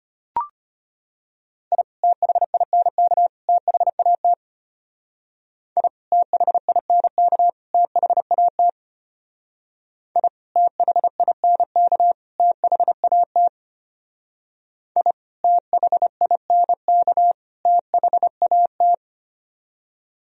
Morse code, heard as a keyed tone, about 32 wpm, 700 Hz. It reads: I THINK THAT I THINK THAT I THINK THAT I THINK THAT